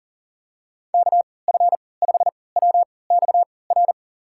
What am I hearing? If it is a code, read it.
Morse code, heard as a keyed tone, about 40 wpm, spaced as for 23 wpm, 700 Hz. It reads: KF5WXR
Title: USA Call Signs - 6 Characters - 3x Character Spacing - 40wpm